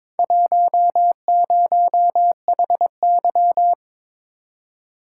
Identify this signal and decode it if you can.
Morse code, heard as a keyed tone, about 23 wpm, 700 Hz. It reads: 10HY
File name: practice / 22wpm / Four Letter-Number Combinations - 22wpm